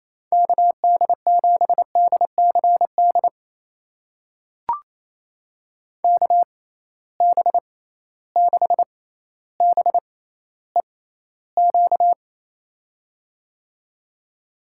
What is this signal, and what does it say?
Morse code, heard as a keyed tone, about 28 wpm, spaced as for 14 wpm, 700 Hz. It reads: KD7DCD KB6BEQ
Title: USA Call Signs - 6 Characters - 6x Character Spacing - 28wpm